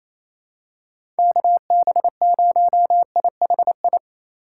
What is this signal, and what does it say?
Morse code, heard as a keyed tone, about 28 wpm, 700 Hz. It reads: KB0IHI
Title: USA Call Signs - 6 Characters - 3x Character Spacing - 28wpm